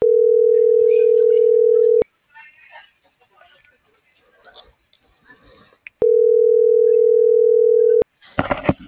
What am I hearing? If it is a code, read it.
Ringback tone (North America), 440 + 480 Hz.